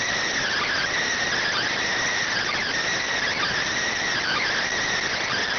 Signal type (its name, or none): Emergency Alert System (SAME) tones